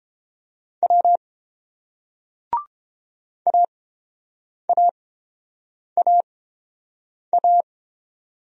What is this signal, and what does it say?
Morse code, heard as a keyed tone, about 31 wpm, spaced as for 7 wpm, 690 Hz. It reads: W AAAA